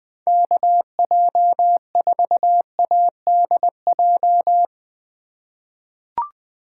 Morse code, heard as a keyed tone, about 20 wpm, 700 Hz. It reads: KJ4ADJ
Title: USA Call Signs - 6 Characters - 2x Character Spacing - 20wpm